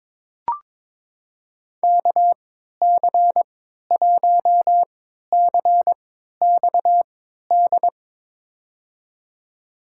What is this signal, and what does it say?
Morse code, heard as a keyed tone, about 21 wpm, spaced as for 12 wpm, 700 Hz. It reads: KC1CXD
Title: USA Call Signs - 6 Characters - 3x Character Spacing - 22wpm